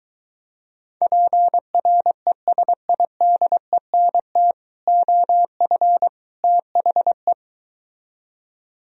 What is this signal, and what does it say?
Morse code, heard as a keyed tone, about 23 wpm, 700 Hz. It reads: PRESIDENT OF THE